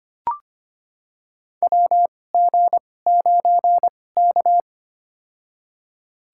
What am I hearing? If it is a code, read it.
Morse code, heard as a keyed tone, about 25 wpm, spaced as for 18 wpm, 700 Hz. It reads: WG9K
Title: USA Call Signs - 4 Characters - 2x Character Spacing - 25wpm